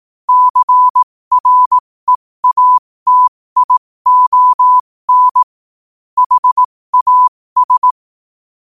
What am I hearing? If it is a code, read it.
Morse code, heard as a keyed tone, about 18 wpm, spaced as for 15 wpm, 1000 Hz. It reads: CREATION HAS
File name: qotdfiles / 15wpm